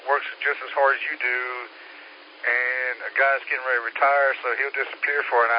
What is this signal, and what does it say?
Emergency Alert System (SAME) tones (header bursts), heard as harsh buzzing bursts, with speech. It is heard from 0.9 to 5.2 s.